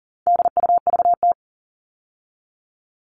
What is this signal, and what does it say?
Morse code, heard as a keyed tone, about 39 wpm, 700 Hz. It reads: DUVT
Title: All Cut Numbers - 4-Digit 40wpm